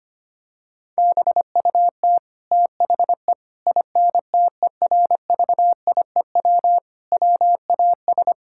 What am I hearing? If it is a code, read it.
Morse code, heard as a keyed tone, about 25 wpm, 700 Hz. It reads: BUT THE INTERVIEW WAS
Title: Sentences from Top 1600 Words 25wpm